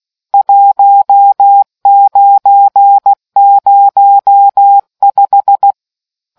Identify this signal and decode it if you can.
Morse code, heard as a keyed tone, about 16 wpm, 800 Hz. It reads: 1905